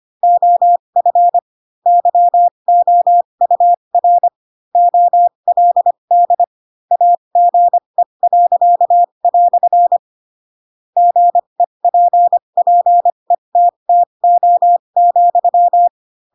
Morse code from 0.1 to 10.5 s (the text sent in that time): OF YOUR OLD AGE."